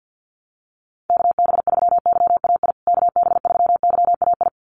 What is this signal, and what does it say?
Morse code, heard as a keyed tone, about 50 wpm, 700 Hz. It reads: KB3YAI KB3YAI
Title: USA Call Signs Repeated 2x 50wpm